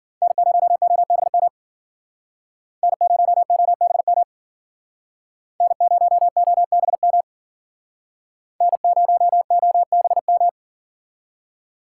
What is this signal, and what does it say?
Morse code, heard as a keyed tone, about 53 wpm, 700 Hz. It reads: N0OBM N0OBM N0OBM N0OBM